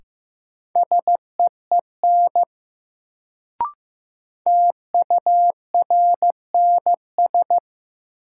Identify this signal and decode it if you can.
Morse code, heard as a keyed tone, about 15 wpm, 700 Hz. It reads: SEEN TURNS
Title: Sets of 5 Words Encore - New Word Drill - 15wpm